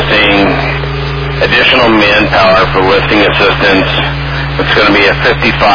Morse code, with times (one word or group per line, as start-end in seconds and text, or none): none